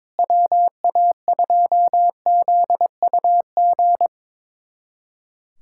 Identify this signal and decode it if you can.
Morse code, heard as a keyed tone, about 22 wpm, 700 Hz. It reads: WA2ZUG
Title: USA Callsign DE Callsign - 6 Characters - 22wpm